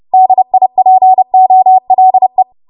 Morse code, heard as a keyed tone, about 30 wpm, 750 Hz. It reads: DIPOLE
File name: Morse / 30WPM